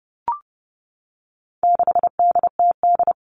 Morse code, heard as a keyed tone, about 30 wpm, 700 Hz. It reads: 6DTD
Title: All Cut Numbers - 4-Digit 30wpm